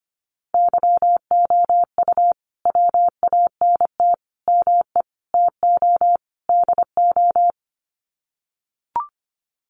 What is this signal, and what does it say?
Morse code, heard as a keyed tone, about 25 wpm, 700 Hz. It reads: YOU WANT ME TO DO
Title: Sets of 5 Words Encore2 25wpm